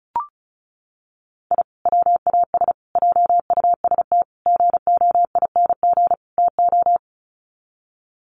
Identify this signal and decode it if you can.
Morse code, heard as a keyed tone, about 34 wpm, 700 Hz. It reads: I WAS JUST GOING TO
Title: Sets of 5 Words Encore 35wpm